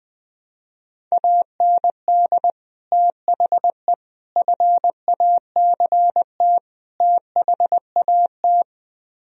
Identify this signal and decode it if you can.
Morse code, heard as a keyed tone, about 20 wpm, 700 Hz. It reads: AND THE FACT THAT